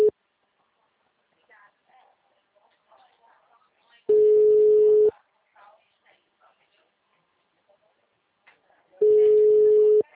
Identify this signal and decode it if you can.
Ringback tone (Europe, 425 Hz), with speech.
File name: out- -2017- - - .54219